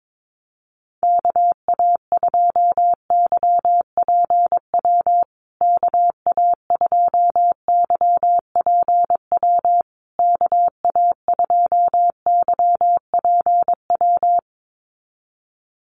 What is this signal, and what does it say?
Morse code, heard as a keyed tone, about 22 wpm, 700 Hz. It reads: KA2YPW KA2YPW KA2YPW
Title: USA Call Signs Repeated 3x 22wpm